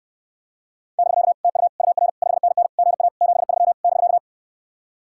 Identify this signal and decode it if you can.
Morse code, heard as a keyed tone, about 69 wpm, 700 Hz. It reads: <BK> TU UR 5NN GA 73 <BK>